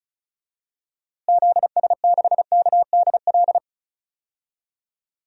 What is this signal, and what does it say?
Morse code, heard as a keyed tone, about 36 wpm, 700 Hz. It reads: ZS6KDL
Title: DXCC-100 Call Signs - 6 Characters - Repeated 2x - 35wpm